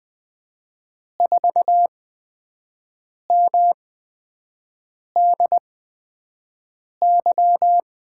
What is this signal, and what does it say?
Morse code, heard as a keyed tone, about 20 wpm, spaced as for 5 wpm, 700 Hz. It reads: 4MDY
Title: USA Call Signs - 8x Character Spacing - 20wpm